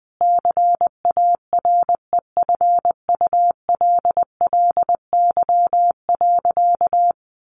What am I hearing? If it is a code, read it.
Morse code, heard as a keyed tone, about 20 wpm, 700 Hz. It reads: CAREFULLY.
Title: Sentences from Top 1300 Words 20wpm